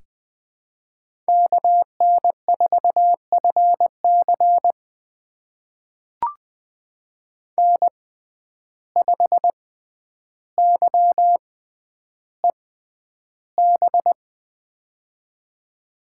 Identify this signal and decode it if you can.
Morse code, heard as a keyed tone, about 20 wpm, spaced as for 10 wpm, 700 Hz. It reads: KN4FC N5YEB
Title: USA Call Signs - 5 Characters - 6x Character Spacing - 20wpm